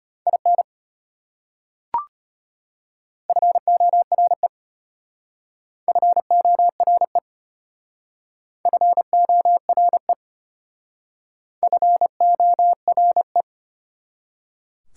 Morse code, heard as a keyed tone, about 30 wpm, 690 Hz. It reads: IN FORE FORE FORE FORE
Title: Speed-Racing Prefixes and Suffixes 25wpm